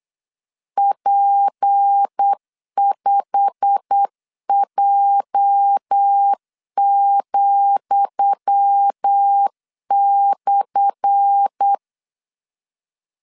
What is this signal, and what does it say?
Morse code, heard as a keyed tone, about 8 wpm, 800 Hz. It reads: P5J,/
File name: LEZIONI / 10 SIGNS / 10D COMA 8wpm 3749